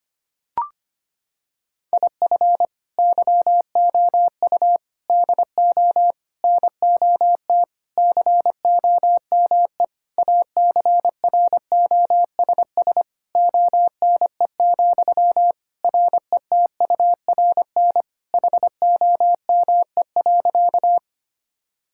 Morse code, heard as a keyed tone, about 25 wpm, 700 Hz. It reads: IF YOU DO NOT COME ACROSS ONE, RETURN HOME.
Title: Sentences from Top 1000 Words - Review Part 5 - 25wpm